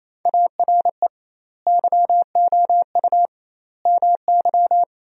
Morse code, heard as a keyed tone, about 28 wpm, 700 Hz. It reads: ARE YOU MY
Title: 2x Word-Spacing - Sentences from Top 300 Words - 28wpm